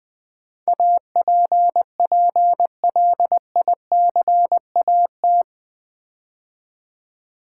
Morse code, heard as a keyed tone, about 20 wpm, 700 Hz. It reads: APPLICAT
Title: Build A Word - 4 Letters At A Time - 11-Letter Words - 20wpm